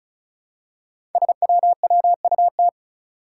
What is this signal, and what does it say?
Morse code, heard as a keyed tone, about 36 wpm, 700 Hz. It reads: SWWUT
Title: Five Challenging Letter-Number Combinations 35wpm